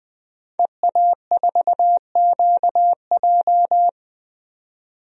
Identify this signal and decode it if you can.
Morse code, heard as a keyed tone, about 20 wpm, 700 Hz. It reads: EA4QJ
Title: DXCC-100 Call Signs - 5 Characters - 20wpm